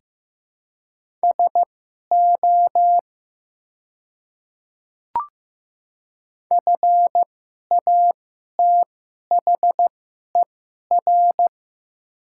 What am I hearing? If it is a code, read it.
Morse code, heard as a keyed tone, about 15 wpm, spaced as for 11 wpm, 700 Hz. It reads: SO FATHER
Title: 2x Character Spacing - Top 200 Words - 15wpm